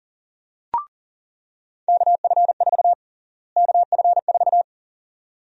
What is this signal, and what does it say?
Morse code, heard as a keyed tone, about 40 wpm, 700 Hz. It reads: KF4 KF4
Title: USA Call Sign Prefixes - Repeated 2x - 40wpm